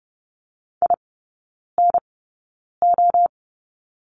Morse code, heard as a keyed tone, about 30 wpm, spaced as for 9 wpm, 700 Hz. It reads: INO
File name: Lesson 16e - Letter c - Three Characters 30wpm